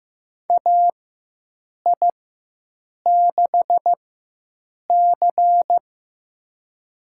Morse code, heard as a keyed tone, about 15 wpm, spaced as for 7 wpm, 700 Hz. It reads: AI6C
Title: USA Call Signs - 4x Character Spacing - 15wpm